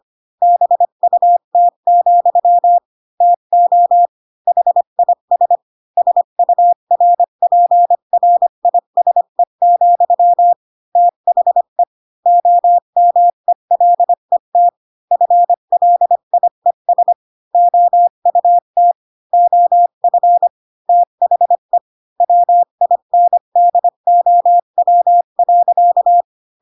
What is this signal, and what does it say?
Morse code, heard as a keyed tone, about 25 wpm, 700 Hz. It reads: BUT, TO HIS SURPRISE, THE OMELET FLIES OUT OF THE WINDOW.
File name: Pinocchio / Pinocchio ch5 20 25 ik1hge